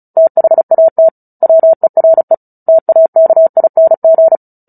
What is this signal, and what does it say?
Morse code, heard as a keyed tone, about 34 wpm, 650 Hz. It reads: THAT WERE TAKING
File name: Book Two / 09-Chapter 9 - 35 WPM